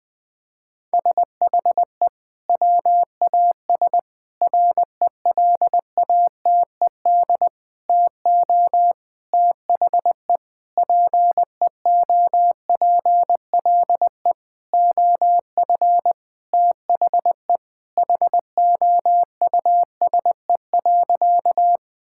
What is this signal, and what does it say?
Morse code, heard as a keyed tone, about 20 wpm, 700 Hz. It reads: SHE WAS RELATED TO THE PEOPLE OF THE HOUSE.
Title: Sentences from Top 1000 Words - Review Part 5 - 20wpm